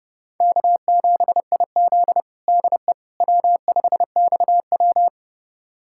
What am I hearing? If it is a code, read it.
Morse code, heard as a keyed tone, about 28 wpm, 700 Hz. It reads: K7IZ DE W5XW
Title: USA Callsign DE Callsign - 4 Characters - 30wpm